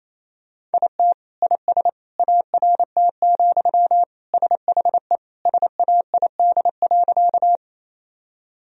Morse code, heard as a keyed tone, about 28 wpm, 700 Hz. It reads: IT IS ART, SHE SAID.
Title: Sentences from Top 1000 Words - Review Part 3 - 28wpm